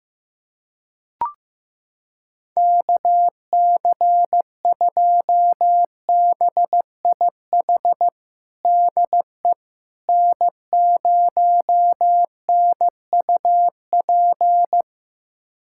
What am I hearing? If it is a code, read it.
Morse code, heard as a keyed tone, about 15 wpm, 700 Hz. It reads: KC2BIH DE N0NUP
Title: USA Callsign DE Callsign 15wpm